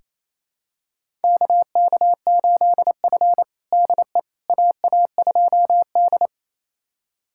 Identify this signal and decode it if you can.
Morse code, heard as a keyed tone, about 27 wpm, 700 Hz. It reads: KK8F DE AA2D
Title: USA Callsign DE Callsign - 4 Characters - 28wpm